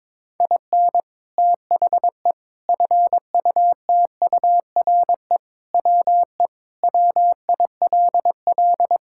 Morse code, heard as a keyed tone, about 22 wpm, 700 Hz. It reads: IN THE FUTURE WE WILL
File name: practice / 22wpm / Sentences with 3-Word Phrases - 22wpm